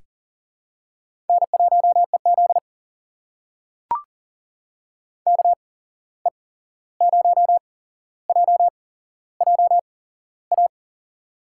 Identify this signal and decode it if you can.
Morse code, heard as a keyed tone, about 40 wpm, spaced as for 15 wpm, 700 Hz. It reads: N1EZ KE0JJA
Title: USA Call Signs - 8x Character Spacing - 40wpm